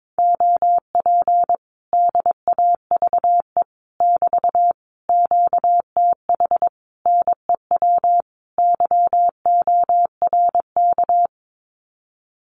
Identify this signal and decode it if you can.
Morse code, heard as a keyed tone, about 22 wpm, 700 Hz. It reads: OP DAVE <BT> QTH NEW YORK
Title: Common QSO Phrases Encore 22wpm